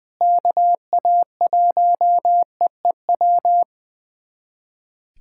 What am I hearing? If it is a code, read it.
Morse code, heard as a keyed tone, about 19 wpm, 700 Hz. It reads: KA1EEW
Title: USA Callsign DE Callsign 20wpm